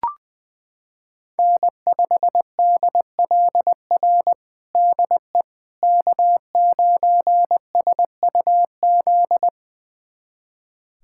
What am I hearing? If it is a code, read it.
Morse code, heard as a keyed tone, about 20 wpm, 700 Hz. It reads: N5DLR DE K9SUZ
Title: USA Callsign DE Callsign - 5 Characters - 20wpm